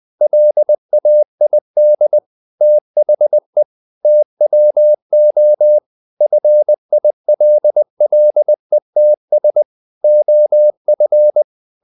Morse code, heard as a keyed tone, about 20 wpm, 600 Hz. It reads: LAID THE TWO FILLETS OF
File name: the old man and the sea part 2 20wpm